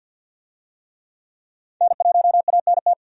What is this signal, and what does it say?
Morse code, heard as a keyed tone, about 46 wpm, 700 Hz. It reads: N1ANT
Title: USA Call Signs - 5 Characters - 50wpm